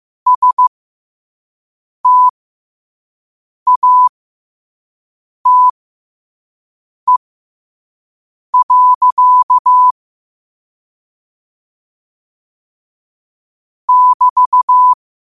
Morse code, heard as a keyed tone, about 15 wpm, spaced as for 5 wpm, 1000 Hz. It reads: STATE. <BT>
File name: qotdfiles / 5wpm